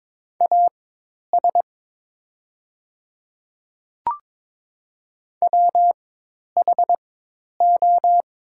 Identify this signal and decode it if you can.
Morse code, heard as a keyed tone, about 22 wpm, spaced as for 10 wpm, 700 Hz. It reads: AS WHO
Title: 4x Character Spacing - Top 100 Words - 22wpm